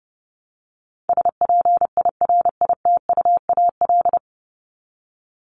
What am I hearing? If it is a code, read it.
Morse code, heard as a keyed tone, about 30 wpm, 700 Hz. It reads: SPIRITUAL
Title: Top 1501-1750 Words 30wpm